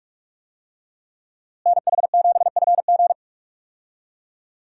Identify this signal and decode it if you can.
Morse code, heard as a keyed tone, about 46 wpm, 700 Hz. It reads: NH7FG
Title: USA Call Signs - 5 Characters - Repeated 2x - 45wpm